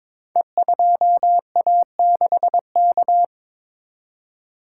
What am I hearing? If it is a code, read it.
Morse code, heard as a keyed tone, about 22 wpm, 700 Hz. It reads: E2A6K
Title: Code Groups - 5 Letter:Numbers - 22wpm